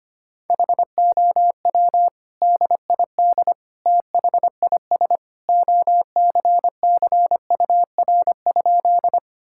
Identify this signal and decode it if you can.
Morse code, heard as a keyed tone, about 25 wpm, 700 Hz. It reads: HOW DID THIS OCCUR?
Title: Sentences from Top 1000 Words - Review Part 6 - 25wpm